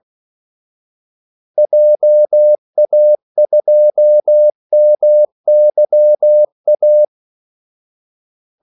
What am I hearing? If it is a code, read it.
Morse code, heard as a keyed tone, about 16 wpm, 600 Hz. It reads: JA2MYA
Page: Kaldesignaler 16 wpm